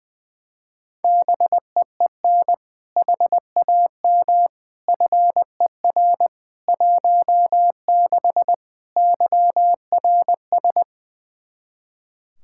Morse code, heard as a keyed tone, about 20 wpm, 700 Hz. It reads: BEEN HAM FER 16 YRS
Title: K1USN SST QSO Practice 20wpm